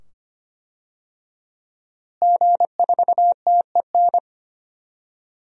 Morse code, heard as a keyed tone, about 25 wpm, 700 Hz. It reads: G4TEN
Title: DXCC-100 Call Signs - 5 Characters - Repeated 2x - 25wpm